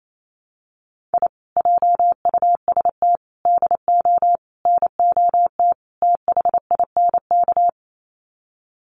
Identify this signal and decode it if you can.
Morse code, heard as a keyed tone, about 28 wpm, 700 Hz. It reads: I JUST DO NOT THINK
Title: Sets of 5 Words Encore 28wpm